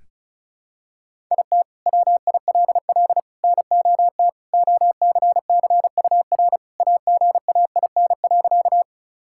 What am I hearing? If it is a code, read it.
Morse code, heard as a keyed tone, about 34 wpm, 700 Hz. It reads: IT WILL NOT OCCUR AGAIN.
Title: Sentences from Top 1000 Words - Review Part 6 - 35wpm